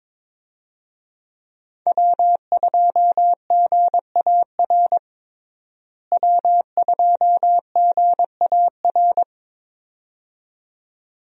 Morse code, heard as a keyed tone, about 22 wpm, 700 Hz. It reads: W2GAR W2GAR
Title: USA Call Signs - 5 Characters - Repeated 2x - 22wpm